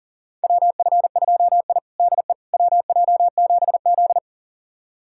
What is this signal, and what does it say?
Morse code, heard as a keyed tone, about 40 wpm, 700 Hz. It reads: WF2I DE WJ7Z
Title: USA Callsign DE Callsign - 4 Characters - 40wpm